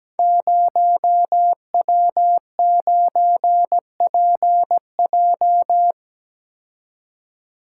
Morse code, heard as a keyed tone, about 17 wpm, 700 Hz. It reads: 0W9PJ
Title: Code Groups - 5 Letter:Numbers - 17wpm